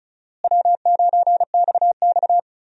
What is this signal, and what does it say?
Morse code, heard as a keyed tone, about 34 wpm, 700 Hz. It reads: W9XX
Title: USA Call Signs - 4 Characters - Repeated 3x - 35wpm